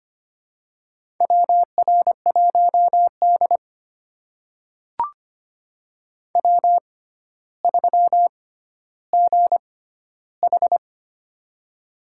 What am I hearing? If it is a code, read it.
Morse code, heard as a keyed tone, about 25 wpm, spaced as for 13 wpm, 700 Hz. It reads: WR1D W3GH
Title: USA Call Signs - 4 Characters - 6x Character Spacing - 25wpm